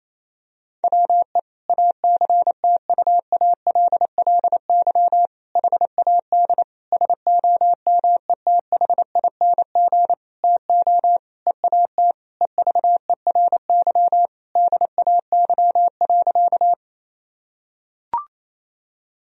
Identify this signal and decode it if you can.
Morse code, heard as a keyed tone, about 28 wpm, 700 Hz. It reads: WE ACTUALLY HAD SOMETHING TO EAT EVERY DAY.
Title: Sentences from Top 1000 Words - Review Part 4 - 28wpm